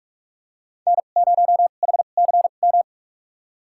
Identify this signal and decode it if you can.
Morse code, heard as a keyed tone, about 45 wpm, spaced as for 32 wpm, 700 Hz. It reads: N0HCM
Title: USA Call Signs - 5 Characters - 2x Character Spacing - 45wpm